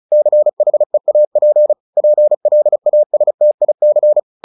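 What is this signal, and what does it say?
Morse code, heard as a keyed tone, about 34 wpm, 600 Hz. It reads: CHEAP PLASTIC
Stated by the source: Morse practice at 35 wpm from CW med Gnister, sending Ken Follett's novel Never in English.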